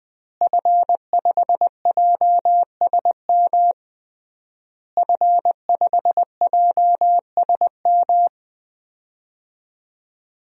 Morse code, heard as a keyed tone, about 20 wpm, 700 Hz. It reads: F5JSM F5JSM
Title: DXCC-100 Call Signs - 5 Characters - Repeated 3x - 20wpm